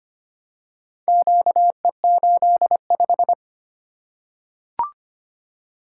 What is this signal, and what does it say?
Morse code, heard as a keyed tone, about 25 wpm, 700 Hz. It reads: QE85
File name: practice / 25wpm / Four Letter-Number Combinations - 25wpm